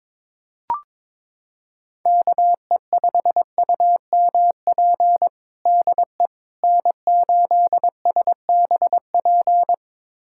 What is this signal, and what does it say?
Morse code, heard as a keyed tone, about 22 wpm, 700 Hz. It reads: KE5UMP DE N8SBP
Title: USA Callsign DE Callsign 22wpm